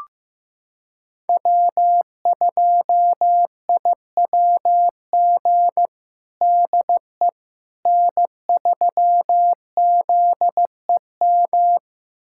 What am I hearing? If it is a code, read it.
Morse code, heard as a keyed tone, about 16 wpm, 700 Hz. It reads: W2IWG DE N3ZEM
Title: USA Callsign DE Callsign - 5 Characters - 15wpm